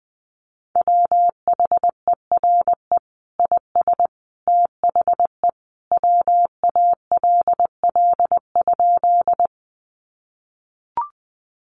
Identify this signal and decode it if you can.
Morse code, heard as a keyed tone, about 20 wpm, 700 Hz. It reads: WHERE IS THE WALL?
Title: Sentences from Top 1100 Words 20wpm